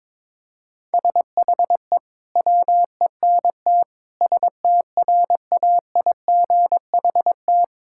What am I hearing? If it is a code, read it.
Morse code, heard as a keyed tone, about 22 wpm, 700 Hz. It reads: SHE WENT STRAIGHT
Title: Sentences from Top 1000 Words - Review Part 6 - 22wpm